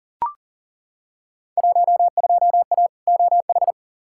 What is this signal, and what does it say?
Morse code, heard as a keyed tone, about 40 wpm, 700 Hz. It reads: 12A OH
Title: Field Day Exchanges Encore - 40wpm